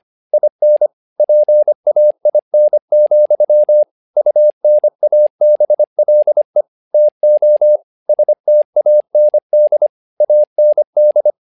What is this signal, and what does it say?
Morse code, heard as a keyed tone, about 25 wpm, 600 Hz. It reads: IN PAIN, UNABLE TO STAND AND
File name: learning center / tower trouble / tower trouble 25wpm